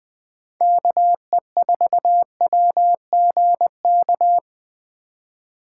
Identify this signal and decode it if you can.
Morse code, heard as a keyed tone, about 20 wpm, 700 Hz. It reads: KE4WGK
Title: USA Call Signs - 6 Characters - 2x Character Spacing - 20wpm